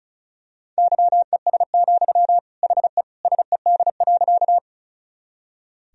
Morse code, heard as a keyed tone, about 36 wpm, 700 Hz. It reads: YES, HE SED.
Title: Sentences from Top 500 Words and Abbreviations 35wpm